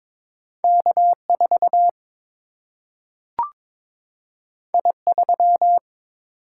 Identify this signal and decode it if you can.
Morse code, heard as a keyed tone, about 22 wpm, 700 Hz. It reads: K4 I3
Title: Single-Letter Single-Number 22wpm